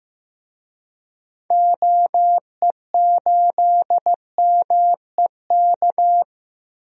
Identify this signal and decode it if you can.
Morse code, heard as a keyed tone, about 15 wpm, 700 Hz. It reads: OE8MEK
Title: DXCC-100 Call Signs - 6 Characters - Repeated 2x - 15wpm